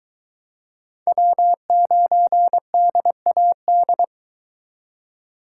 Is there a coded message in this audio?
Yes: Morse code from 1.1 to 4.0 s, heard as a keyed tone, about 22 wpm, 700 Hz. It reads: W9DAD